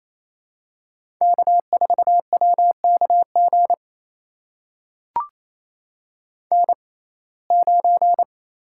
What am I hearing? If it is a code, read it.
Morse code, heard as a keyed tone, about 28 wpm, spaced as for 20 wpm, 700 Hz. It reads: K4WKG N9
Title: USA Call Signs - 5 Characters - 6x Character Spacing - 28wpm